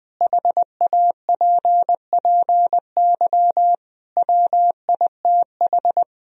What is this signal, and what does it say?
Morse code, heard as a keyed tone, about 20 wpm, 700 Hz. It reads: HAPPY WITH
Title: Sentences from Top 1000 Words - Review Part 2 - 20wpm